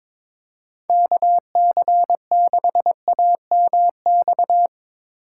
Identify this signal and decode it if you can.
Morse code, heard as a keyed tone, about 22 wpm, 700 Hz. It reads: KC6AMX
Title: USA Call Signs - 6 Characters - 4x Character Spacing - 22wpm